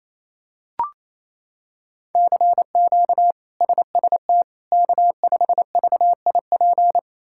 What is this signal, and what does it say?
Morse code, heard as a keyed tone, about 28 wpm, 700 Hz. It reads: CQ SST K5VIP
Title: K1USN SST QSO Practice - 28wpm